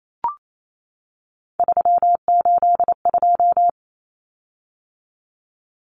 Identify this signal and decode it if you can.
Morse code, heard as a keyed tone, about 28 wpm, 700 Hz. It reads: 382
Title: Common Cut Numbers - 3-Digit 28wpm